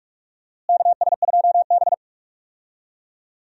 Morse code, heard as a keyed tone, about 46 wpm, 700 Hz. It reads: KS2B
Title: USA Call Signs - 4 Characters - Repeated 3x - 45wpm